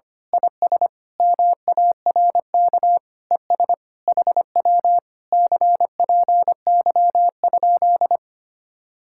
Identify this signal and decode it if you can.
Morse code, heard as a keyed tone, about 24 wpm, 700 Hz. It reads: IS MARK ES HW CPY?
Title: Speed-Racing Common QSO Phrases - 25wpm